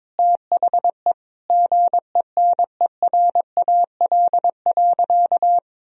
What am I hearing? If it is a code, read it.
Morse code, heard as a keyed tone, about 22 wpm, 700 Hz. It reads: THE GENERAL.
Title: Sentences from Top 1000 Words - Review Part 3 - 22wpm